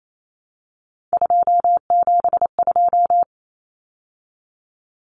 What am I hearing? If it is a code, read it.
Morse code, heard as a keyed tone, about 28 wpm, 700 Hz. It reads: 272
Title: Numbers - 3-Digit 28wpm